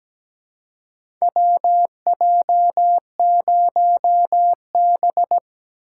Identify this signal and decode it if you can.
Morse code, heard as a keyed tone, about 17 wpm, 700 Hz. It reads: WJ0B
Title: USA Call Signs - 4 Characters - 2x Character Spacing - 17wpm